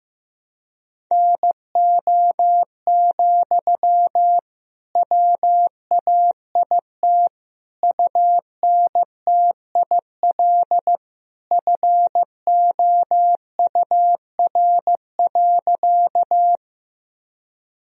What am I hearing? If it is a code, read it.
Morse code, heard as a keyed tone, about 15 wpm, 700 Hz. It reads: NO, WAIT UNTIL FOUR.
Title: Sentences from Top 1000 Words - Review Part 1 - 15wpm